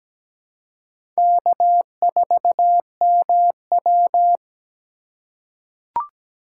Morse code, heard as a keyed tone, about 17 wpm, 700 Hz. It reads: K4MW
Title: USA Call Signs - 2x Character Spacing - 17wpm